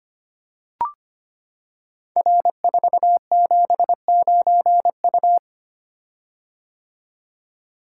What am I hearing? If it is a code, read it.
Morse code, heard as a keyed tone, about 25 wpm, 700 Hz. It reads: R479U
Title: Code Groups - 5 Letter:Numbers - 25wpm